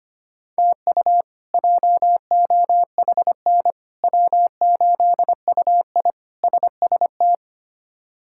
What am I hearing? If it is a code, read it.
Morse code, heard as a keyed tone, about 25 wpm, 700 Hz. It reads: TU JOHN W8UI SST
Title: K1USN SST QSO Practice - 25wpm